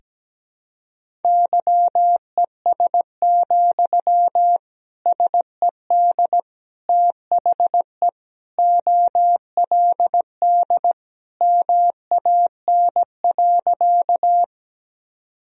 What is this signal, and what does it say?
Morse code, heard as a keyed tone, about 17 wpm, 700 Hz. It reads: YES, SED THE OLD MAN.
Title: Sentences from Top 500 Words and Abbreviations 17wpm